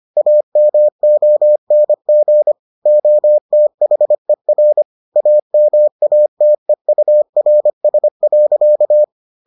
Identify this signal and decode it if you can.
Morse code, heard as a keyed tone, about 25 wpm, 600 Hz. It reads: AMONG OTHER AMATEURS.